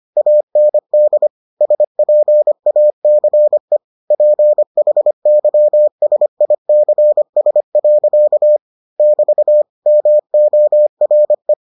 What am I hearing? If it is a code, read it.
Morse code, heard as a keyed tone, about 25 wpm, 600 Hz. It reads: AND SPACE PHYSICS. <BT> MORE